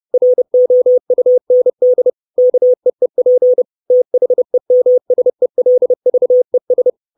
Morse code, heard as a keyed tone, about 30 wpm, 500 Hz. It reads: ROUND KEEP THEMSELVES
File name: 632 Words 30WPM